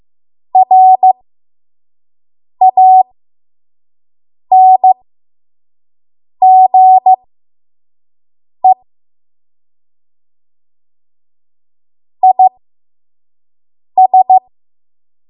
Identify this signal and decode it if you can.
Morse code, heard as a keyed tone, about 15 wpm, spaced as for 5 wpm, 750 Hz. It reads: RANGE IS
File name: Morse / 05WPM